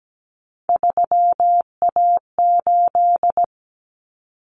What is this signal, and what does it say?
Morse code, heard as a keyed tone, about 17 wpm, 690 Hz. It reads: 3A8
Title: Common Cut Numbers - 3-Digit 17wpm